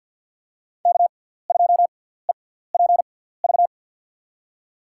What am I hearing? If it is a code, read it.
Morse code, heard as a keyed tone, about 53 wpm, spaced as for 17 wpm, 700 Hz. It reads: K2EPV
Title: USA Call Signs - 5 Characters - 6x Character Spacing - 50wpm